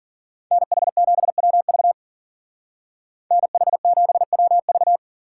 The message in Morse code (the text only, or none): NH7WV NH7WV